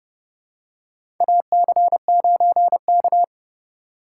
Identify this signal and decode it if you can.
Morse code, heard as a keyed tone, about 30 wpm, 700 Hz. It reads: AC9K
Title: USA Call Signs - 4 Characters - 4x Character Spacing - 30wpm